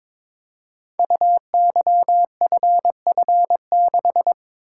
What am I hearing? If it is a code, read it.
Morse code, heard as a keyed tone, about 22 wpm, 700 Hz. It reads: UYFF6
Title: Code Groups - 5 Letter:Numbers - 22wpm